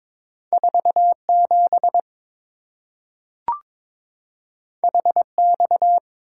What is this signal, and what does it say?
Morse code, heard as a keyed tone, about 22 wpm, 690 Hz. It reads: 47 HX
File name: practice / 22wpm / Two Letter-Number Combinations - 22wpm